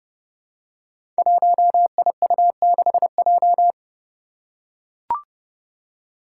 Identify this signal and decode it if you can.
Morse code, heard as a keyed tone, about 30 wpm, 700 Hz. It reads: 1IU6J
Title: Five Challenging Letter-Number Combinations 30wpm